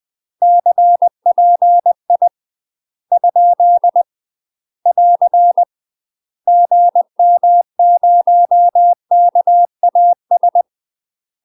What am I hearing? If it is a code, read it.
Morse code, heard as a keyed tone, about 20 wpm, 700 Hz. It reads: CPI ? <AR> GM0KAS
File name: UK MASTER QSOs TRAINING / 20WPM / UK MASTER QSOs 0049